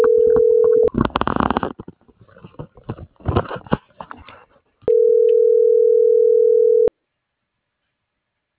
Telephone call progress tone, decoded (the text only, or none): ringback tone